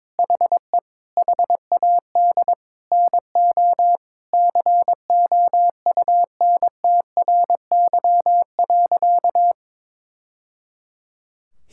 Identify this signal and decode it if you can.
Morse code, heard as a keyed tone, about 22 wpm, 700 Hz. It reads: HE HAD NO COUNTRY.
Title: Sentences from Top 1000 Words - Review Part 5 - 22wpm